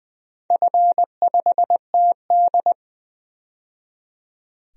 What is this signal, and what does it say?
Morse code, heard as a keyed tone, about 20 wpm, 700 Hz. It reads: F5TD
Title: DXCC-100 Call Signs - 4 Characters - 20wpm